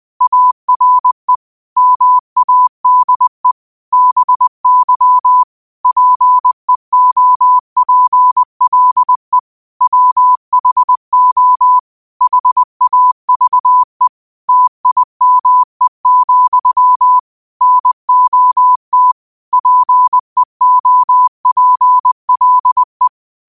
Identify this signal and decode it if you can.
Morse code, heard as a keyed tone, about 20 wpm, 1000 Hz. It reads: ARE MADE BY PEOPLE WHO HAVE TIME, NOT PEOPLE